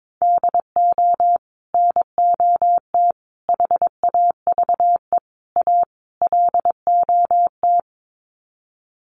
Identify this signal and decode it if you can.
Morse code, heard as a keyed tone, about 22 wpm, 700 Hz. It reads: DO NOT HAVE A LOT
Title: Sets of 5 Words Encore 22wpm